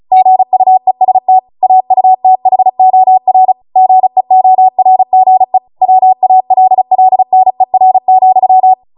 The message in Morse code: GUEST AUTHOR GEORGE WALLNER,